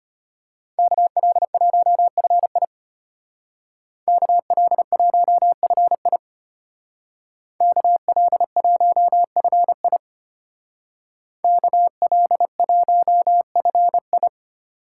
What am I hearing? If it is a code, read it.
Morse code, heard as a keyed tone, about 33 wpm, 700 Hz. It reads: KL1FI KL1FI KL1FI KL1FI